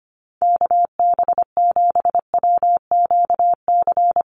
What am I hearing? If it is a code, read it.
Morse code, heard as a keyed tone, about 25 wpm, 700 Hz. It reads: KB7WQC
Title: USA Call Signs Repeated 2x 25wpm